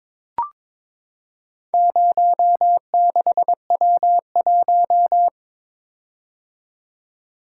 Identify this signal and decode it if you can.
Morse code, heard as a keyed tone, about 23 wpm, 700 Hz. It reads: 06W1
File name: practice / 22wpm / Four Letter-Number Combinations - 22wpm